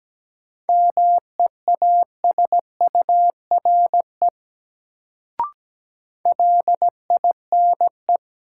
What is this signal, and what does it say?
Morse code, heard as a keyed tone, about 17 wpm, 700 Hz. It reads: MEASURE LINE
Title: Sentences from Top 1000 Words - Review Part 1 - New Word Drill - 17wpm